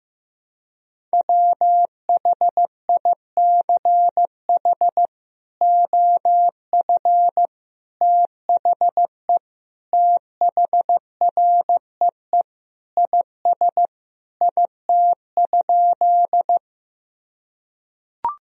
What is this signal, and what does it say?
Morse code, heard as a keyed tone, about 15 wpm, 700 Hz. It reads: WHICH OF THE THREE IS IT?
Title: Sentences from Top 1000 Words - Review Part 1 - 15wpm